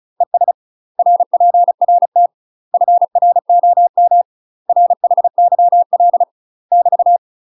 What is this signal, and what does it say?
Morse code, heard as a keyed tone, about 35 wpm, 700 Hz. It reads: ES RPRT FROM RHYL <BT>
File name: UK MASTER QSOs TRAINING / 35WPM / UK MASTER QSOs 0041